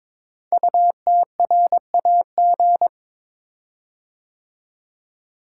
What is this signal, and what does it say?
Morse code, heard as a keyed tone, about 22 wpm, 690 Hz. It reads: UTRAG
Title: Code Groups - 5 Letter:Numbers - 22wpm